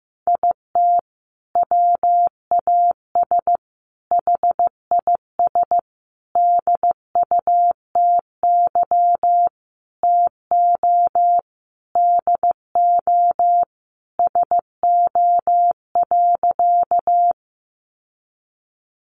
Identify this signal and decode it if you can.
Morse code, heard as a keyed tone, about 15 wpm, 700 Hz. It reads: IT WAS HIS DUTY TO DO SO.
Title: Sentences from Top 1500 Words 15wpm